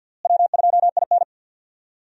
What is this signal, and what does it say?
Morse code, heard as a keyed tone, about 46 wpm, 700 Hz. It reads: W2IN
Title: USA Call Signs - 4 Characters - Repeated 3x - 50wpm